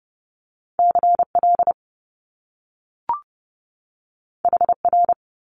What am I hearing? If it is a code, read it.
Morse code, heard as a keyed tone, about 30 wpm, 700 Hz. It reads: CL HR
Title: Top Two Letter Combinations in Words 30wpm